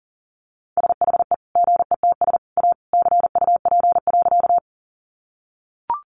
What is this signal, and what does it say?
Morse code, heard as a keyed tone, about 40 wpm, 700 Hz. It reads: SHE GETS A CUP.
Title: Easy Phrases 40wpm